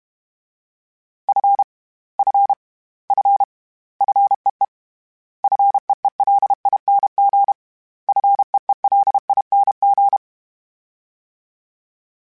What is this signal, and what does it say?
Morse code, heard as a keyed tone, about 32 wpm, 800 Hz. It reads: F F F FEE FEELING FEELING
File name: Short Runs / Short Runs-mp3s / SinglePrefixes @30